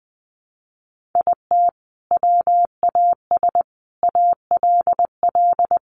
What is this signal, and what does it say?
Morse code, heard as a keyed tone, about 19 wpm, 700 Hz. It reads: IT WAS ALL
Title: Sentences from Top 600 Words 20wpm